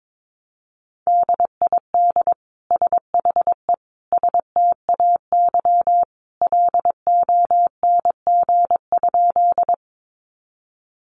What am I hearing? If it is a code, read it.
Morse code, heard as a keyed tone, about 22 wpm, 700 Hz. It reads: DID SHE STAY LONG?
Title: Sentences from Top 500 Words 22wpm